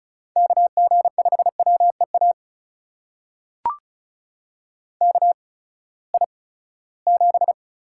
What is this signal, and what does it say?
Morse code, heard as a keyed tone, about 36 wpm, spaced as for 20 wpm, 700 Hz. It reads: KG5WEA KI7
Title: USA Call Signs - 6 Characters - 8x Character Spacing - 35wpm